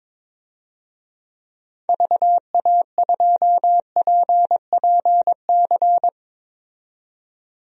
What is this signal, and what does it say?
Morse code, heard as a keyed tone, about 22 wpm, 700 Hz. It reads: VA2PPC
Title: DXCC-100 Call Signs - 6 Characters - 22wpm